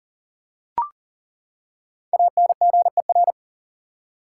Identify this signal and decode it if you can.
Morse code, heard as a keyed tone, about 40 wpm, 700 Hz. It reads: ANGER
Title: Five Letter Words - 40wpm